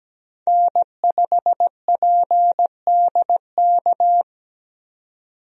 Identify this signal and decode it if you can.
Morse code, heard as a keyed tone, about 17 wpm, 700 Hz. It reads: N5PDK
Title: USA Call Signs - 5 Characters - 17wpm